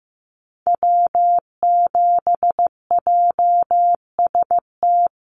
Morse code, heard as a keyed tone, about 14 wpm, 700 Hz. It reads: W7JST
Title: USA Call Signs Repeated 2x 15wpm